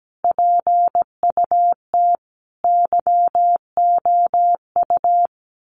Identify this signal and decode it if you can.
Morse code, heard as a keyed tone, about 17 wpm, 700 Hz. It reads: PUT YOU
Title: Sentences from Top 1100 Words 17wpm